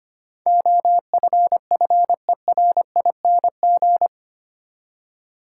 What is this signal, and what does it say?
Morse code, heard as a keyed tone, about 25 wpm, 700 Hz. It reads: OFFERING
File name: 8 Letter Words - First 3 Letters Given - 25wpm